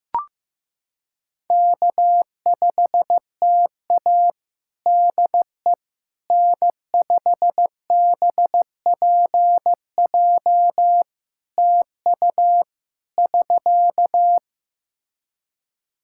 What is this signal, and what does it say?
Morse code, heard as a keyed tone, about 15 wpm, 700 Hz. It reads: K5TA DE N5BPJ TU <SK>
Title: K1USN SST QSO Practice 15wpm